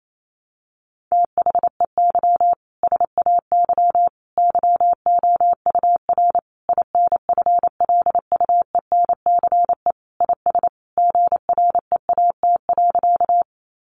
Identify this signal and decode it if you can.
Morse code, heard as a keyed tone, about 28 wpm, 700 Hz. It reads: THEY SAY YOUR INFLUENCE IS GREAT.
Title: Sentences from Top 1100 Words 28wpm